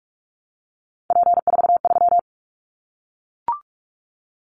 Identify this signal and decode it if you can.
Morse code, heard as a keyed tone, about 44 wpm, 700 Hz. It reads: P43
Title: Callsign Prefixes for Top 100 DXCC - 45wpm